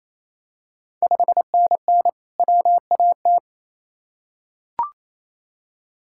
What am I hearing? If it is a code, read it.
Morse code, heard as a keyed tone, about 28 wpm, 700 Hz. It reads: 5NN WAT
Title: Ontario QSO Party Exchange 28wpm